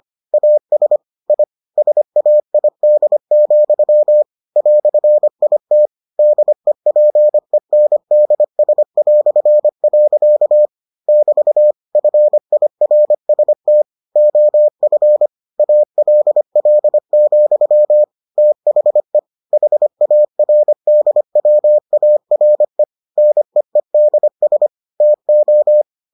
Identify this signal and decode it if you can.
Morse code, heard as a keyed tone, about 25 wpm, 600 Hz. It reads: AS I SAID, "IT DEPENDS". <BT> FIRST OF ALL, THE HARDWARE NEEDS TO